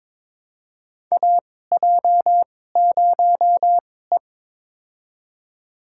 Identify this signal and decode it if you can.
Morse code, heard as a keyed tone, about 21 wpm, spaced as for 16 wpm, 700 Hz. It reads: AJ0E
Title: USA Call Signs - 2x Character Spacing - 22wpm